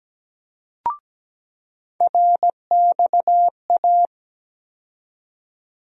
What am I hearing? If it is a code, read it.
Morse code, heard as a keyed tone, about 17 wpm, 700 Hz. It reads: RXA
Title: USA Call Sign Postfixes - 3 Letters 17wpm